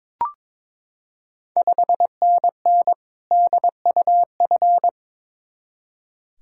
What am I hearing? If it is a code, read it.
Morse code, heard as a keyed tone, about 22 wpm, 690 Hz. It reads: 5NN DUF
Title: Ontario QSO Party Exchange 22wpm